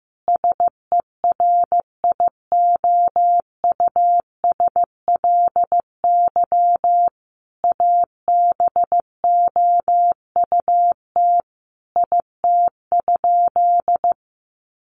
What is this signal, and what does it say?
Morse code, heard as a keyed tone, about 14 wpm, 700 Hz. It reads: SERIOUSLY ABOUT IT?
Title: Sentences from Top 1200 Words 15wpm